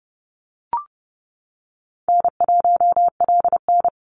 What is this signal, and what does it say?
Morse code, heard as a keyed tone, about 30 wpm, 700 Hz. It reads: N1LN
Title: Easy Call Signs - 30wpm